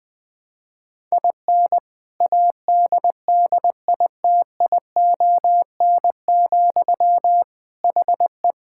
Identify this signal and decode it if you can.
Morse code, heard as a keyed tone, about 20 wpm, 700 Hz. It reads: IN ADDITION, HE
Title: Sentences from Top 1000 Words - Review Part 6 - 20wpm